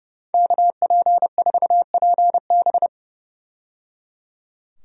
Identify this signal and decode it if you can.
Morse code, heard as a keyed tone, about 29 wpm, 700 Hz. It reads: KP4PB
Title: DXCC-100 Call Signs - 5 Characters - 30wpm